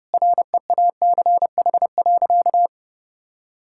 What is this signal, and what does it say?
Morse code, heard as a keyed tone, about 30 wpm, 700 Hz. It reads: REACH.
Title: Sentences from Top 1000 Words - Review Part 6 - 30wpm